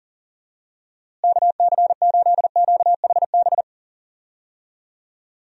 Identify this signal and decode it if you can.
Morse code, heard as a keyed tone, about 40 wpm, 700 Hz. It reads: KC8QHB
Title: USA Call Signs - 6 Characters - Repeated 3x - 40wpm